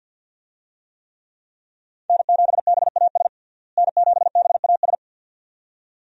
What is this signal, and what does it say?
Morse code, heard as a keyed tone, about 47 wpm, 700 Hz. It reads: N7BAS N7BAS
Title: USA Call Signs - 5 Characters - Repeated 2x - 50wpm